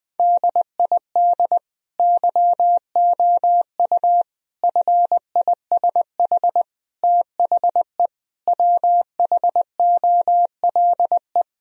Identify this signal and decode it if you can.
Morse code, heard as a keyed tone, about 19 wpm, 700 Hz. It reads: DID YOU FISH THE WHOLE
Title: Sentences from Top 1000 Words - Review Part 1 - 20wpm